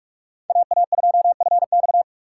Morse code, heard as a keyed tone, about 46 wpm, 700 Hz. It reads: AA2FX
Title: USA Call Signs - 5 Characters - 2x Character Spacing - 45wpm